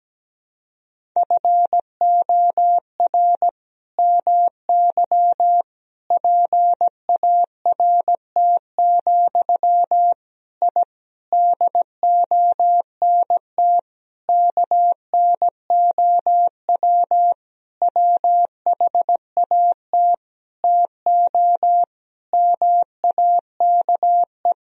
Morse code, heard as a keyed tone, about 17 wpm, 700 Hz. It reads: FOR MY PART, I DONT KNOW WHAT TO MAKE
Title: Sentences from Top 1000 Words - Review Part 1 - 17wpm